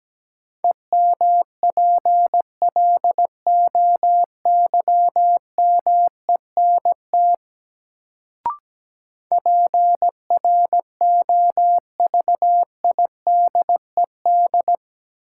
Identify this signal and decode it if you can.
Morse code, heard as a keyed tone, about 17 wpm, 700 Hz. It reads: EMPLOYMENT PROVIDED
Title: Sentences from Top 1500 Words - New Word Drill Alt - 17wpm